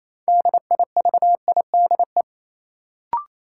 Morse code, heard as a keyed tone, about 27 wpm, 700 Hz. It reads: DIVIDE
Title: Sentences from Top 1600 Words - New Word Drill Alt - 28wpm